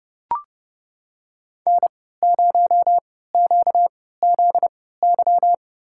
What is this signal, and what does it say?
Morse code, heard as a keyed tone, about 30 wpm, spaced as for 17 wpm, 700 Hz. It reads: N0QZY
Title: USA Call Signs - 5 Characters - 3x Character Spacing - 30wpm